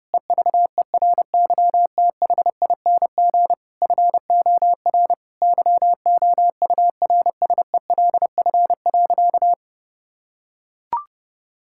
Morse code, heard as a keyed tone, about 30 wpm, 700 Hz. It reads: EVERYTHING FOR YOURSELF.
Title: Sentences from Top 1000 Words - Review Part 6 - 30wpm